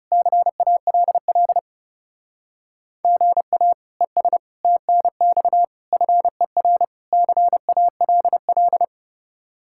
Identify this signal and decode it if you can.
Morse code, heard as a keyed tone, about 35 wpm, 700 Hz. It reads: CALL GA ES TNX FER CALL
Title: Speed-Racing Common QSO Phrases - 30wpm